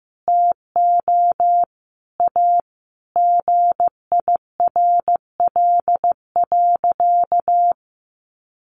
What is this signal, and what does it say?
Morse code, heard as a keyed tone, about 15 wpm, 700 Hz. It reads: TO A GIRL.
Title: Sentences from Top 1100 Words 15wpm